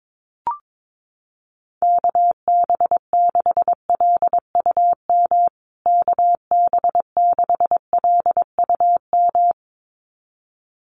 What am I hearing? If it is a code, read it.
Morse code, heard as a keyed tone, about 22 wpm, 700 Hz. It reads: KB6LUM KB6LUM
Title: USA Call Signs Repeated 2x 22wpm